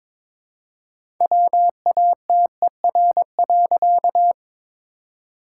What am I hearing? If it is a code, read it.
Morse code, heard as a keyed tone, about 23 wpm, 690 Hz. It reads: WATER.
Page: morse-code-ninja-practice-22wpm